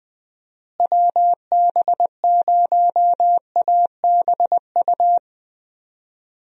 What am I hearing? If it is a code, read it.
Morse code, heard as a keyed tone, about 20 wpm, 700 Hz. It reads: WB0ABU
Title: USA Call Signs - 6 Characters - Repeated 3x - 20wpm